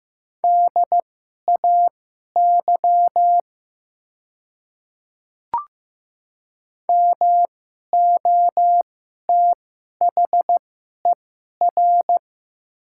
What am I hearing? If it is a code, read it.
Morse code, heard as a keyed tone, about 15 wpm, spaced as for 11 wpm, 700 Hz. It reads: DAY MOTHER
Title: 2x Character Spacing - Top 200 Words - 15wpm